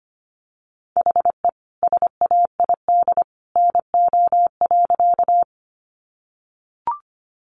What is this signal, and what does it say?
Morse code, heard as a keyed tone, about 25 wpm, 700 Hz. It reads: HE SAID NO.
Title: Sentences from Top 100 Words 25wpm